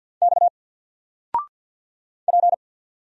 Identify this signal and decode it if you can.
Morse code, heard as a keyed tone, about 50 wpm, 700 Hz. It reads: X P
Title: Single Character (Letter:Number:Punctuation:Prosign) 50wpm